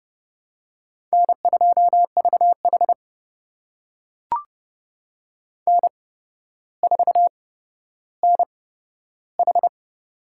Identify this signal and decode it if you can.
Morse code, heard as a keyed tone, about 30 wpm, spaced as for 13 wpm, 700 Hz. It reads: N2VH N4NH
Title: USA Call Signs - 4 Characters - 8x Character Spacing - 30wpm